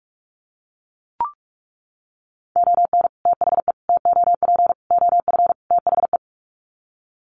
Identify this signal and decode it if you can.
Morse code, heard as a keyed tone, about 44 wpm, 700 Hz. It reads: ON THE TOP OF THE
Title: Sets of 5 Words Encore 45wpm